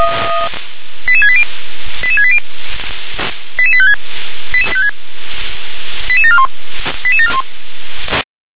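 Five-tone selective call (ZVEI), heard as quick runs of five tones, tones of about 70 ms. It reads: 88680 88680 88856 88741 88741
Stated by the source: ZVEI five-tone selective call received by SDR.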